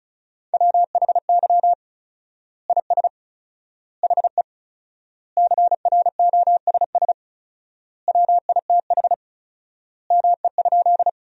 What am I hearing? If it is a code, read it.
Morse code, heard as a keyed tone, about 36 wpm, 700 Hz. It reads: WHY IS HE CROSS WITH ME?
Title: 4x Word-Spacing - Sentences from Top 300 Words - 35wpm